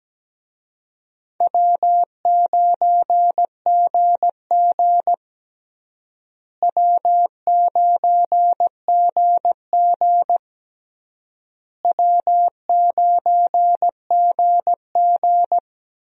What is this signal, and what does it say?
Morse code, heard as a keyed tone, about 17 wpm, 700 Hz. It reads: W9GG W9GG W9GG
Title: USA Call Signs - 4 Characters - Repeated 3x - 17wpm